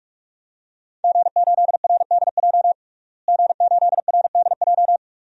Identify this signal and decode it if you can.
Morse code, heard as a keyed tone, about 43 wpm, 700 Hz. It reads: G8RDJ G8RDJ
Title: DXCC-100 Call Signs - 5 Characters - Repeated 2x - 45wpm